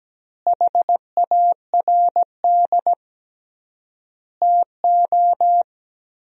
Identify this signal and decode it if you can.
Morse code, heard as a keyed tone, about 17 wpm, 700 Hz. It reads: HARD TO
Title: 3x Word-Spacing - Sentences from Top 300 Words - 17wpm